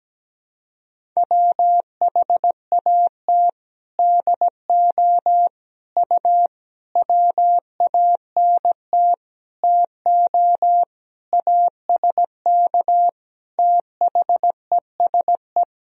Morse code, heard as a keyed tone, about 17 wpm, 700 Hz. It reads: WHAT DO U WANT TO ASK THESE
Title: Sentences from Top 200 Words and Abbreviations 17wpm